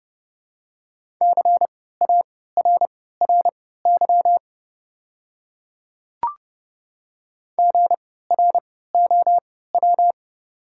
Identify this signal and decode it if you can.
Morse code, heard as a keyed tone, about 30 wpm, spaced as for 17 wpm, 700 Hz. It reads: CARRY GROW
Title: 3x Character Spacing - Top 300 Words - 30wpm